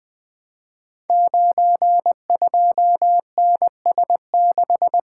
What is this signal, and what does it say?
Morse code, heard as a keyed tone, about 20 wpm, 700 Hz. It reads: 92NS6
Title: Code Groups - 5 Letter:Numbers - 20wpm